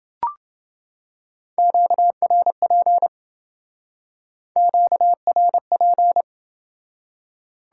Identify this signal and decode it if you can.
Morse code, heard as a keyed tone, about 28 wpm, 700 Hz. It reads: QRP QRP